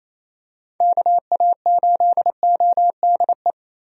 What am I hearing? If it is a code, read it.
Morse code, heard as a keyed tone, about 28 wpm, 700 Hz. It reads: KA8ODE
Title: USA Call Signs - 6 Characters - Repeated 2x - 28wpm